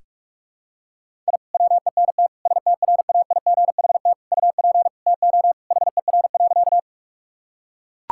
Morse code, heard as a keyed tone, about 41 wpm, 700 Hz. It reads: I WENT STRAIGHT UP TO HER.
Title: Sentences from Top 1000 Words - Review Part 6 - 45wpm